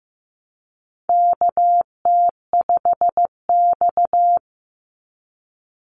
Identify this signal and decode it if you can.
Morse code, heard as a keyed tone, about 15 wpm, 700 Hz. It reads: KT5X
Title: NA SOTA Chaser Callsigns 15wpm